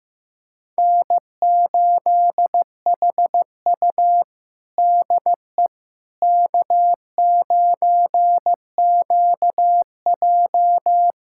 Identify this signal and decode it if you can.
Morse code, heard as a keyed tone, about 15 wpm, 700 Hz. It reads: N8HU DE K9QJ
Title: USA Callsign DE Callsign - 4 Characters - 15wpm